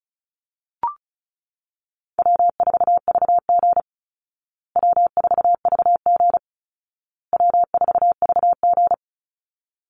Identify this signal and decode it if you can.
Morse code, heard as a keyed tone, about 34 wpm, 700 Hz. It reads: W4VG W4VG W4VG
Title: Easy Call Signs Repeated 3x - 35wpm